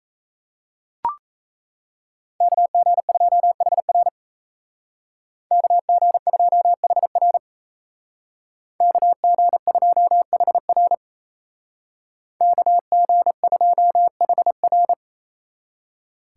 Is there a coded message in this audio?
Yes: Morse code from 2.4 to 14.9 s, heard as a keyed tone, about 37 wpm, 700 Hz. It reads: KG2HR KG2HR KG2HR KG2HR